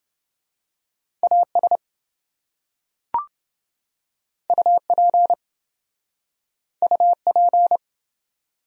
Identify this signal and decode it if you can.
Morse code, heard as a keyed tone, about 26 wpm, 700 Hz. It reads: AS UP UP